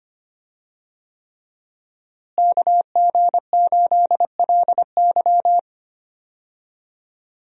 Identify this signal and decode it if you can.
Morse code, heard as a keyed tone, about 25 wpm, 700 Hz. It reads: KG8LY
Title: USA Call Signs - 5 Characters - Repeated 3x - 25wpm